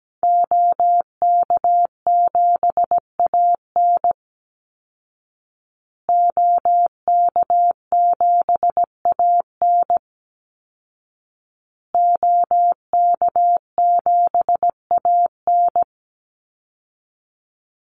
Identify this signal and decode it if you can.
Morse code, heard as a keyed tone, about 17 wpm, 700 Hz. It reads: OK7AN OK7AN OK7AN
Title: Call Signs Repeated 2x - 17wpm